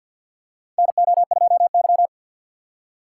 Morse code, heard as a keyed tone, about 51 wpm, 700 Hz. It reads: NO2Y
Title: USA Call Signs - 4 Characters - 3x Character Spacing - 50wpm